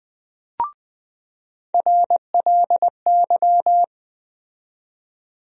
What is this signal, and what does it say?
Morse code, heard as a keyed tone, about 21 wpm, 700 Hz. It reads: RLY
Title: USA Call Sign Postfixes - 3 Letters 20wpm